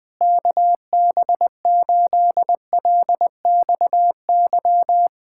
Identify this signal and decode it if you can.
Morse code, heard as a keyed tone, about 20 wpm, 700 Hz. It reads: KB8LXY
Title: Speed-Racing Common QSO Phrases Encore - 20wpm